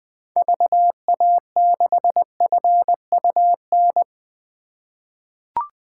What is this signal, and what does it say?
Morse code, heard as a keyed tone, about 20 wpm, 700 Hz. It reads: VA6FUN
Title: 2023 SOTA Chaser Call Signs - 20wpm